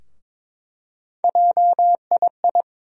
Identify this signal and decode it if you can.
Morse code, heard as a keyed tone, about 22 wpm, 700 Hz. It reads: JII
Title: USA Call Sign Postfixes - 3 Letters 22wpm